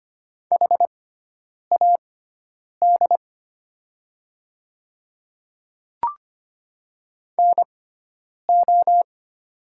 Morse code, heard as a keyed tone, about 25 wpm, spaced as for 9 wpm, 700 Hz. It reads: HAD NO
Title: 6x Character Spacing - Top 200 Words - 25wpm